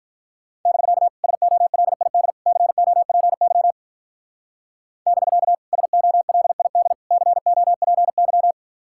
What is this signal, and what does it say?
Morse code, heard as a keyed tone, about 43 wpm, 700 Hz. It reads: <BK> SOLID COPY <BK> SOLID COPY
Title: Speed-Racing Common QSO Phrases Encore - 35wpm